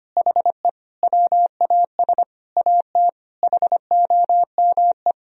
Morse code, heard as a keyed tone, about 25 wpm, 700 Hz. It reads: HE WAS AT HOME
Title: Sentences from Top 1000 Words - Review Part 2 - 25wpm